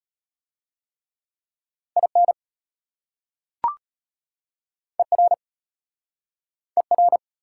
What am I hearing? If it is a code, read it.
Morse code, heard as a keyed tone, about 33 wpm, 700 Hz. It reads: IN ER ER